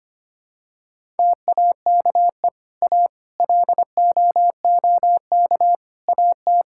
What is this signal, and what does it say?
Morse code, heard as a keyed tone, about 25 wpm, 700 Hz. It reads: TAKE A LOOK AT
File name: practice / 25wpm / Sentences with 4-Word Phrases - 25wpm